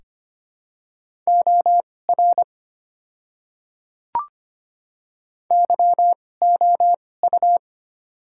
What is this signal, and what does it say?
Morse code, heard as a keyed tone, about 25 wpm, spaced as for 18 wpm, 700 Hz. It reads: OR YOU
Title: 2x Character Spacing - Top 100 Words - 25wpm